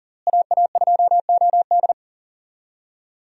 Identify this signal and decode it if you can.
Morse code, heard as a keyed tone, about 40 wpm, 700 Hz. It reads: AA2OD
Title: USA Call Signs - 5 Characters - Repeated 3x - 40wpm